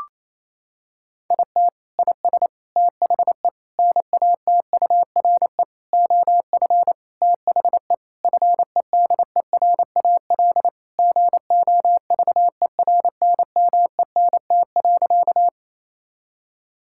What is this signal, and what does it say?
Morse code, heard as a keyed tone, about 27 wpm, 700 Hz. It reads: IT IS THE NATURE OF THE FEDERAL GOVERNMENT.
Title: Sentences from Top 1000 Words - Review Part 5 - 28wpm